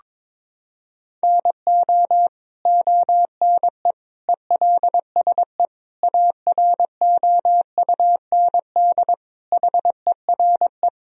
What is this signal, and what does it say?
Morse code, heard as a keyed tone, about 22 wpm, 700 Hz. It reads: NO ONE ELSE AROUND HERE
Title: Sentences from Top 1000 Words - Review Part 3 - 22wpm